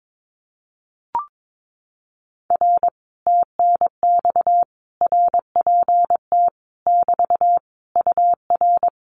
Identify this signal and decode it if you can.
Morse code, heard as a keyed tone, about 22 wpm, 700 Hz. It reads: R TNX RPT <BT> UR
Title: Common QSO Phrases Encore 22wpm